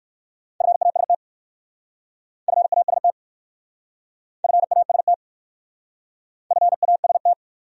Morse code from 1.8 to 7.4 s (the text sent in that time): FAST FAST FAST